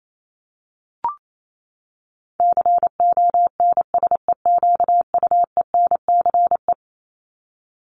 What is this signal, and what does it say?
Morse code, heard as a keyed tone, about 28 wpm, 700 Hz. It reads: CONSEQUENCE
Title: LY Suffix - Top 100 Words with Base Words 28wpm